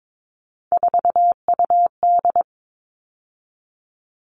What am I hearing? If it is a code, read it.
Morse code, heard as a keyed tone, about 22 wpm, 700 Hz. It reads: 4UD
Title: All Cut Numbers - 3-Digit 22wpm